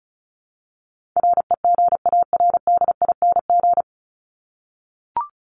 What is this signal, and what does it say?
Morse code, heard as a keyed tone, about 34 wpm, 700 Hz. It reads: REGARDING
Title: ING Suffix - Top 100 Words with Base Words 35wpm